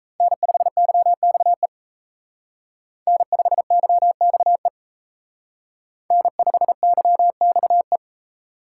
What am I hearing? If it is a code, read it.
Morse code, heard as a keyed tone, about 35 wpm, 700 Hz. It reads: N5YXE N5YXE N5YXE
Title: Speed Racing - USA 1x3 Call Signs 28wpm